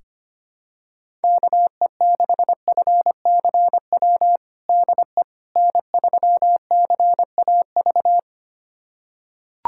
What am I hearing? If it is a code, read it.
Morse code, heard as a keyed tone, about 25 wpm, 700 Hz. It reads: KE6FCW DE N3CAV
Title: USA Callsign DE Callsign 25wpm